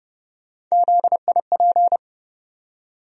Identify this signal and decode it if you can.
Morse code, heard as a keyed tone, about 32 wpm, 700 Hz. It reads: ZIP
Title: USA Call Sign Postfixes - 3 Letters 30wpm